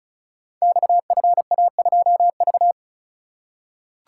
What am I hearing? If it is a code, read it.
Morse code, heard as a keyed tone, about 36 wpm, 700 Hz. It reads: XFA2V
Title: Code Groups - 5 Letter:Numbers - 35wpm